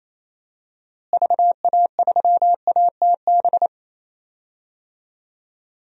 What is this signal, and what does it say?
Morse code, heard as a keyed tone, about 28 wpm, 700 Hz. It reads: VA3ATB
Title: DXCC-100 Call Signs - 6 Characters - Repeated 3x - 28wpm